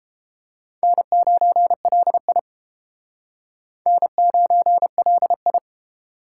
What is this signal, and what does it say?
Morse code, heard as a keyed tone, about 29 wpm, 700 Hz. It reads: N9LI N9LI